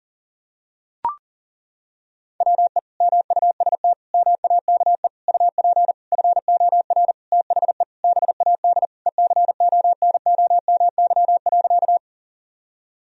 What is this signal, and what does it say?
Morse code, heard as a keyed tone, about 40 wpm, 700 Hz. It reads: WE MUST MAKE UP FOR THE BAD ECONOMY.
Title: Sentences from Top 1000 Words - Review Part 3 - 40wpm